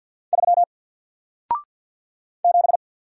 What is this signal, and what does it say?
Morse code, heard as a keyed tone, about 50 wpm, 700 Hz. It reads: 3 7
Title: Single Letter-Number Alt 50wpm